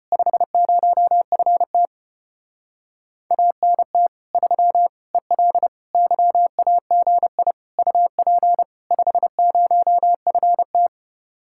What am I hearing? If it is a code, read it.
Morse code, heard as a keyed tone, about 33 wpm, 700 Hz. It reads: 50FT ANT 3 EL YAGI UP 50FT